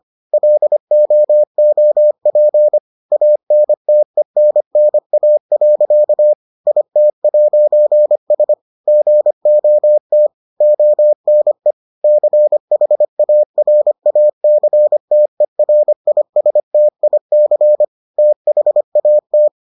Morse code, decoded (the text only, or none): LOOP ANTENNA. IT'S GOT ONE CHARACTERISTIC THAT